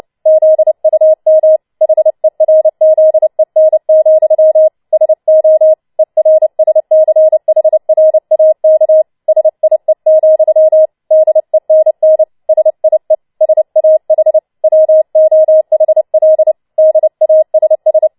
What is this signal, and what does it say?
Morse code, heard as a keyed tone, about 29 wpm, 630 Hz. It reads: ZUM HERZEN, SO ERSCHRAK SIE, DENN SIE SAH WOHL DASS